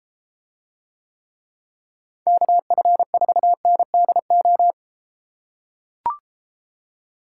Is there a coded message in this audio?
Yes: Morse code from 2.3 to 4.7 s, heard as a keyed tone, about 33 wpm, 700 Hz. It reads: KF4NDO